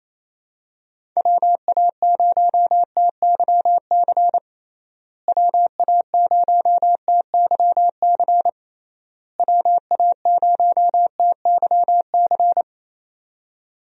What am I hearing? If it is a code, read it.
Morse code, heard as a keyed tone, about 28 wpm, 700 Hz. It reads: WA0TYC WA0TYC WA0TYC
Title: USA Call Signs - 6 Characters - Repeated 3x - 28wpm